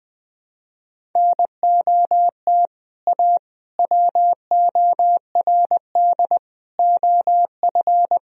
Morse code, heard as a keyed tone, about 20 wpm, 700 Hz. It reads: NOT A WORD OF
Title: Sentences from Top 1000 Words - Review Part 1 - 20wpm